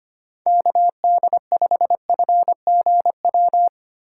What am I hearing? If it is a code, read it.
Morse code, heard as a keyed tone, about 25 wpm, 700 Hz. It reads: KD5FGW
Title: USA Call Signs - 6 Characters - 3x Character Spacing - 25wpm